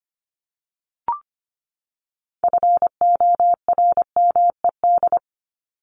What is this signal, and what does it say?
Morse code, heard as a keyed tone, about 25 wpm, 700 Hz. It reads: FORMED
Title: ED Suffix - Top 100 Words with Base Words 25wpm